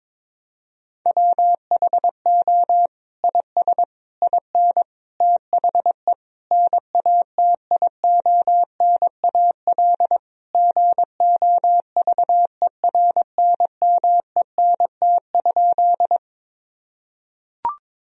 Morse code, heard as a keyed tone, about 22 wpm, 700 Hz. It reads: WHO IS IN THE NATIONAL GOVERNMENT?
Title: Sentences from Top 1000 Words - Review Part 4 - 22wpm